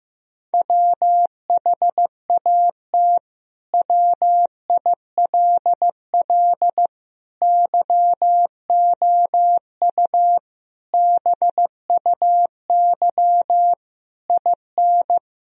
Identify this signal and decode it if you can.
Morse code, heard as a keyed tone, about 16 wpm, 700 Hz. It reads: WHAT WILL YOU BUY IN
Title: Sentences from Top 1000 Words - Review Part 4 - 15wpm